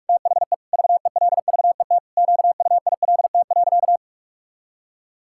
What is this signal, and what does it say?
Morse code, heard as a keyed tone, about 46 wpm, 700 Hz. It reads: THE VELVET QUILT.
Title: Pangram Challenge v2 - 45wpm